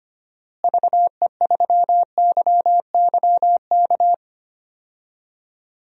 Morse code, heard as a keyed tone, about 25 wpm, 700 Hz. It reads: VE3YYK
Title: DXCC-100 Call Signs - 6 Characters - Repeated 3x - 25wpm